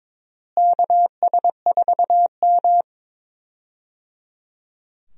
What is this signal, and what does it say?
Morse code, heard as a keyed tone, about 22 wpm, 700 Hz. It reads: KS4M
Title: USA Call Signs - 4 Characters - Repeated 3x - 22wpm